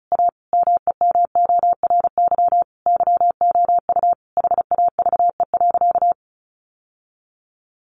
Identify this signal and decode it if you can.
Morse code, heard as a keyed tone, about 34 wpm, 700 Hz. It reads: A MEMORY YOU HAVE.
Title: Sentences from Top 1300 Words 35wpm